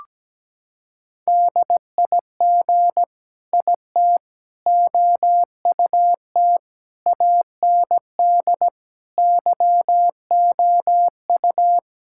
Morse code, heard as a keyed tone, about 17 wpm, 700 Hz. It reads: DIG IT OUT AND YOU
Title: Sentences from Top 1600 Words 17wpm